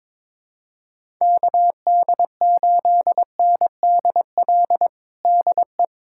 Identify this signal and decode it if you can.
Morse code, heard as a keyed tone, about 22 wpm, 700 Hz. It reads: KD8NDL DE
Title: USA Callsign DE Callsign 22wpm